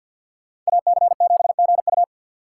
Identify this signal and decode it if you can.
Morse code, heard as a keyed tone, about 63 wpm, 700 Hz. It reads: AC7GU